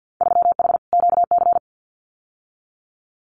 Morse code, heard as a keyed tone, about 50 wpm, 700 Hz. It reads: 3H QC
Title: Winter Field Day Exchanges Encore 50wpm